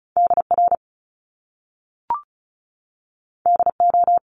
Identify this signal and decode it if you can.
Morse code, heard as a keyed tone, about 36 wpm, 700 Hz. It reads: DR DO
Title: Top Two Letter Combinations in Words 35wpm